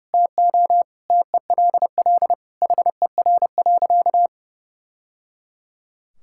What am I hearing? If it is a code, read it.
Morse code, heard as a keyed tone, about 30 wpm, 700 Hz. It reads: TO TELL HER.
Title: Sentences from Top 1000 Words - Review Part 5 - 30wpm